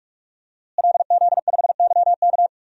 Morse code, heard as a keyed tone, about 42 wpm, 700 Hz. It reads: LZ5YK
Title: DXCC-100 Call Signs - 5 Characters - Repeated 2x - 45wpm